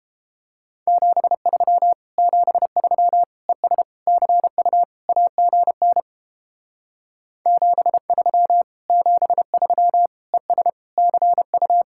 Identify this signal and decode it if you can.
Morse code, heard as a keyed tone, about 29 wpm, 700 Hz. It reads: 73 73 ES CU AGN 73 73 ES CU